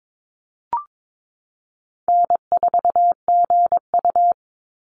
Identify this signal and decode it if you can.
Morse code, heard as a keyed tone, about 22 wpm, 700 Hz. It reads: N4GU
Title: Easy Call Signs - 22wpm